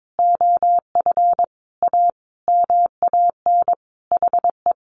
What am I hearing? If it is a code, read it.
Morse code, heard as a keyed tone, about 22 wpm, 700 Hz. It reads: OF A MAN HE
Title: Sentences from Top 1300 Words 22wpm